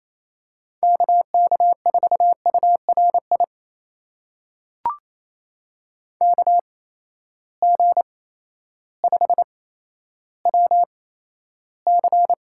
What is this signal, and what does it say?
Morse code, heard as a keyed tone, about 28 wpm, spaced as for 13 wpm, 700 Hz. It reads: KK4URI KG5WC
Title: USA Call Signs - 6 Characters - 8x Character Spacing - 28wpm